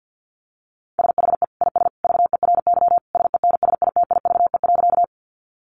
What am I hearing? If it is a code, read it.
Morse code, heard as a keyed tone, about 48 wpm, 700 Hz. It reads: SHE IS VERY SENSITIVE.
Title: Sentences from Top 1400 Words 50wpm